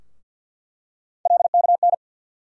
Morse code, heard as a keyed tone, about 47 wpm, 700 Hz. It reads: LKN
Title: USA Call Sign Postfixes - 3 Letters 50wpm